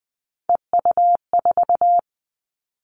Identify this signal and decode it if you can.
Morse code, heard as a keyed tone, about 20 wpm, 700 Hz. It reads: EU4
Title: Callsign Prefixes for Top 100 DXCC - 20wpm